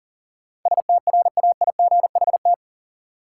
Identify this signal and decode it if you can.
Morse code, heard as a keyed tone, about 40 wpm, 700 Hz. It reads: STRAIGHT
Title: Sentences from Top 1000 Words - New Word Drill - 40wpm